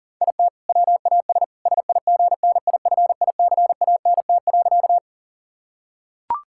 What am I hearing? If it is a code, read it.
Morse code, heard as a keyed tone, about 40 wpm, 690 Hz. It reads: IT WAS SIGNIFICANT.
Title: Sentences from Top 1000 Words - Review Part 6 - 40wpm